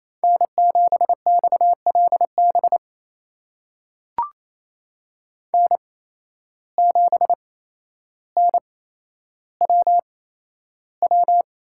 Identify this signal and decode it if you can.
Morse code, heard as a keyed tone, about 28 wpm, spaced as for 12 wpm, 700 Hz. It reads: N7XLB N7NWW
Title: USA Call Signs - 5 Characters - 8x Character Spacing - 28wpm